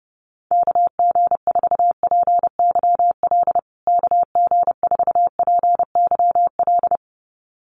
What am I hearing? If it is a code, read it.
Morse code, heard as a keyed tone, about 30 wpm, 700 Hz. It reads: KG4PYL KG4PYL
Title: USA Call Signs Repeated 2x 30wpm